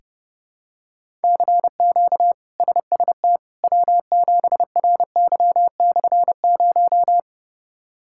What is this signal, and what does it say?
Morse code, heard as a keyed tone, about 30 wpm, 700 Hz. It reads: CQ SST W7RY/0
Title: K1USN SST QSO Practice - 30wpm